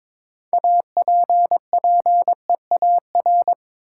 Morse code, heard as a keyed tone, about 22 wpm, 700 Hz. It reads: APPEAR
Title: Sentences from Top 1000 Words - Review Part 2 - New Word Drill - 22wpm